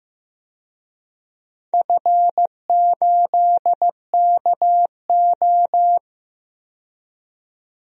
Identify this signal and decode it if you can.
Morse code, heard as a keyed tone, about 15 wpm, 700 Hz. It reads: F8KO
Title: DXCC-100 Call Signs - 4 Characters - 15wpm